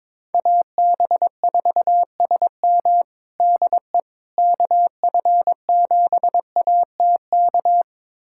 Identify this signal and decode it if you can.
Morse code, heard as a keyed tone, about 22 wpm, 700 Hz. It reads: AB4SM DE KF7ATK
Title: USA Callsign DE Callsign 22wpm